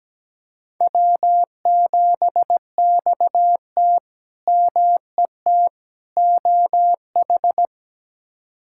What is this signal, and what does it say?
Morse code, heard as a keyed tone, about 17 wpm, 700 Hz. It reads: W7XT MET OH
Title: K1USN SST QSO Practice - 17wpm